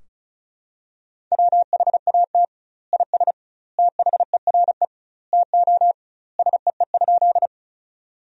Morse code, heard as a keyed tone, about 36 wpm, 700 Hz. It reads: WHAT IS THERE TO SEE?
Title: 2x Word-Spacing - Sentences from Top 100 Words - 35wpm